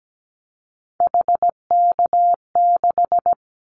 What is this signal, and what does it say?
Morse code, heard as a keyed tone, about 17 wpm, 700 Hz. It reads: HK6
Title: Callsign Prefixes for Top 100 DXCC - 17wpm